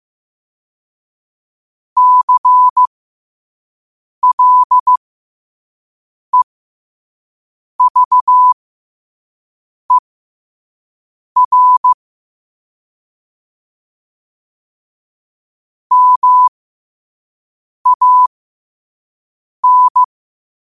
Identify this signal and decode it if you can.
Morse code, heard as a keyed tone, about 15 wpm, spaced as for 5 wpm, 1000 Hz. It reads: CLEVER MAN
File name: qotdfiles / 5wpm